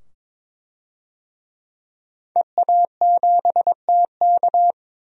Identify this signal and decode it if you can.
Morse code, heard as a keyed tone, about 22 wpm, 700 Hz. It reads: EA7TK
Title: DXCC-100 Call Signs - 5 Characters - 22wpm